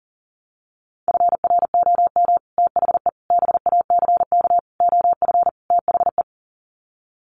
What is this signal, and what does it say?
Morse code, heard as a keyed tone, about 40 wpm, 700 Hz. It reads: FROM THE BACK OF THE
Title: Sets of 5 Words Encore 40wpm